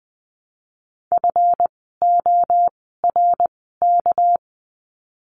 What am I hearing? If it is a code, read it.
Morse code, heard as a keyed tone, about 20 wpm, spaced as for 14 wpm, 700 Hz. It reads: FORK
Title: 199 Lesson 32f - Letter k - Words 20wpm